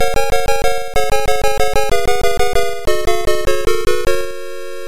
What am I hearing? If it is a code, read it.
There is no code.